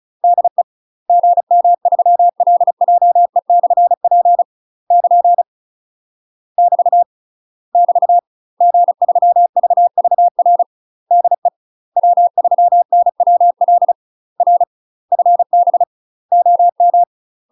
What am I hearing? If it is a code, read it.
Morse code, heard as a keyed tone, about 36 wpm, 700 Hz. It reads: DE GM3LJE/P <KN> <BT> <BT> G3VVR DE W3NWL R FB OM
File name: UK MASTER QSOs TRAINING / 35WPM / UK MASTER QSOs 0038